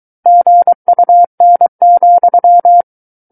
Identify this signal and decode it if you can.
Morse code, heard as a keyed tone, about 23 wpm, 700 Hz. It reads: GUN,
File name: Book One / 14-Chapter 14 - 23 WPM